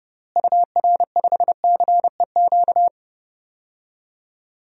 Morse code, heard as a keyed tone, about 30 wpm, 700 Hz. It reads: UR5CEQ
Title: DXCC-100 Call Signs - 6 Characters - Repeated 2x - 30wpm